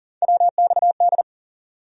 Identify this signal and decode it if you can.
Morse code, heard as a keyed tone, about 37 wpm, 700 Hz. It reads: WXD
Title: USA Call Sign Postfixes - 3 Letters 40wpm